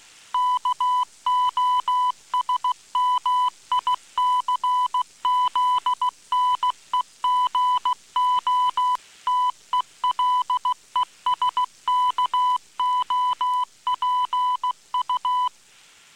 Morse code, heard as a keyed tone, about 16 wpm, 1000 Hz. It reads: KOSMICZNEGO TELESKOPU